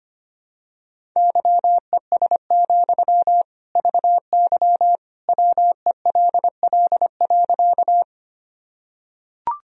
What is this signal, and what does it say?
Morse code, heard as a keyed tone, about 25 wpm, 700 Hz. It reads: YES, VY WELL.
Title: Sentences from Top 500 Words and Abbreviations 25wpm